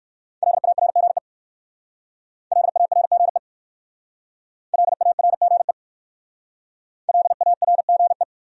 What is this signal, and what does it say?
Morse code, heard as a keyed tone, about 53 wpm, 700 Hz. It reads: LARGE LARGE LARGE LARGE
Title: Speed-Racing Top 200 Words 45wpm